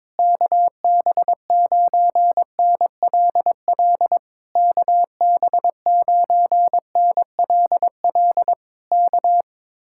Morse code, heard as a keyed tone, about 22 wpm, 700 Hz. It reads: KB9NLL KB9NLL K
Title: Speed-Racing Common QSO Phrases - 22wpm